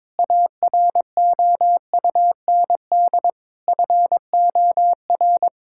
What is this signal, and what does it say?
Morse code, heard as a keyed tone, about 22 wpm, 700 Hz. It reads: AROUND FOR
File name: Sentences with 7-Word Phrases - 22wpm